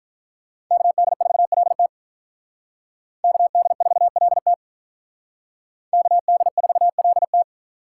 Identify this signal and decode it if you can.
Morse code, heard as a keyed tone, about 46 wpm, 700 Hz. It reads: KD4LT KD4LT KD4LT